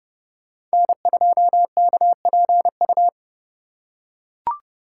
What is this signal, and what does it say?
Morse code, heard as a keyed tone, about 30 wpm, 700 Hz. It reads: N2KPU
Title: USA Call Signs - 5 Characters - 3x Character Spacing - 30wpm